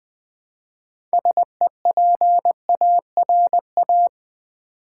Morse code, heard as a keyed tone, about 20 wpm, 700 Hz. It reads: SEPARA
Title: Build A Word - 2 Letters At A Time - 8-Letter Words - 20wpm